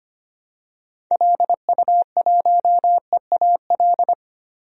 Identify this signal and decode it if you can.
Morse code, heard as a keyed tone, about 25 wpm, 700 Hz. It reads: LU1EAL
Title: DXCC-100 Call Signs - 6 Characters - 25wpm